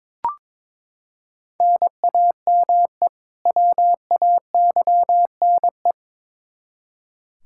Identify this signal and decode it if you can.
Morse code, heard as a keyed tone, about 22 wpm, 700 Hz. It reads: NAME WAYNE
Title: Practice QSO - Exchange 1 - Beginning - 22wpm